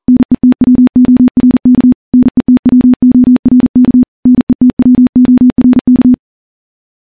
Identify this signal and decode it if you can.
Morse code, heard as a keyed tone, about 40 wpm, 250 Hz. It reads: NETWORK NETWORK NETWORK
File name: morse / morse-training-40WPM